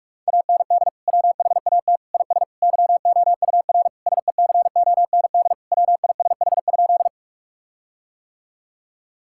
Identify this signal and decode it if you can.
Morse code, heard as a keyed tone, about 46 wpm, 700 Hz. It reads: AND WHAT IS YOUR SECOND WISH?
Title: Sentences from Top 1600 Words 45wpm